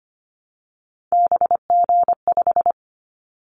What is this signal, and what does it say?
Morse code, heard as a keyed tone, about 25 wpm, 700 Hz. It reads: BG5
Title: Callsign Prefixes for Top 100 DXCC - 25wpm